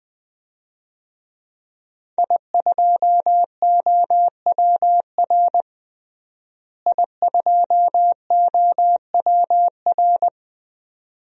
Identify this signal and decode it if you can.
Morse code, heard as a keyed tone, about 20 wpm, 700 Hz. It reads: I2OWR I2OWR
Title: DXCC-100 Call Signs - 5 Characters - Repeated 2x - 20wpm